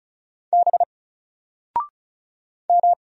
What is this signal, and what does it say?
Morse code, heard as a keyed tone, about 36 wpm, 700 Hz. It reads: B M
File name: Single-Letter-Number-35wpm